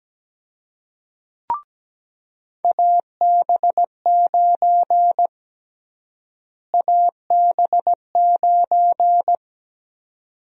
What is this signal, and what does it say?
Morse code, heard as a keyed tone, about 17 wpm, 700 Hz. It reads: AB9 AB9
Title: USA Call Sign Prefixes - Repeated 2x - 17wpm